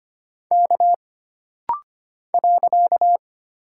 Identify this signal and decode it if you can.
Morse code, heard as a keyed tone, about 25 wpm, 700 Hz. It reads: K .
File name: Single Character (Letter:Number:Punctuation:Prosign) - Rapid-Fire - 25wpm